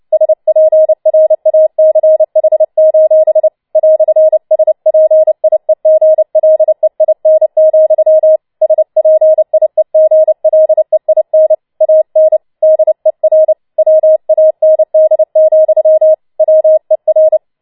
Morse code, heard as a keyed tone, about 29 wpm, 630 Hz. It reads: SPRACH: "SPIEGLEIN, SPIEGLEIN AN DER WAND, WER
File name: Schneewittchen Teil3 30wpm